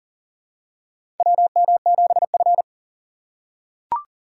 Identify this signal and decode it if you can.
Morse code, heard as a keyed tone, about 40 wpm, 700 Hz. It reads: WM7F
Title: USA Call Signs - 4 Characters - 6x Character Spacing - 40wpm